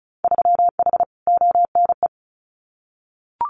Morse code, heard as a keyed tone, about 36 wpm, 700 Hz. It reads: 3H ONE
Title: Winter Field Day Exchanges Encore 35wpm